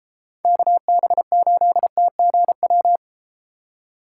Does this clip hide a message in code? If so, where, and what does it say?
Yes: Morse code from 0.5 to 3.0 s, heard as a keyed tone, about 32 wpm, 700 Hz. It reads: KB8TGW